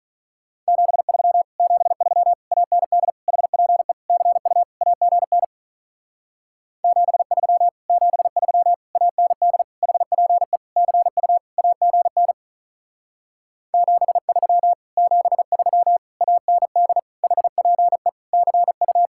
Morse code, heard as a keyed tone, about 47 wpm, 700 Hz. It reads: 73 73 AND HPE CU AGN 73 73 AND HPE CU AGN 73 73 AND HPE CU